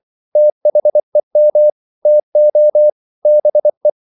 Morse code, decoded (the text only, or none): THEM TO BE